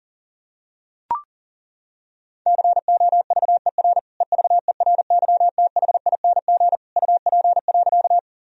Morse code, heard as a keyed tone, about 40 wpm, 700 Hz. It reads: COVER EVERYTHING UP.
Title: Sentences from Top 1000 Words - Review Part 1 - 40wpm